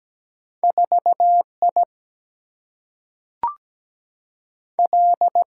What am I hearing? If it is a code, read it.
Morse code, heard as a keyed tone, about 17 wpm, 700 Hz. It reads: 4I L
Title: Single-Letter Single-Number 17wpm